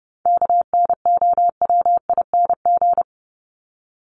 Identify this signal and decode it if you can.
Morse code, heard as a keyed tone, about 30 wpm, 700 Hz. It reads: KNOWING
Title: ING Suffix - Top 100 Words with Base Words 30wpm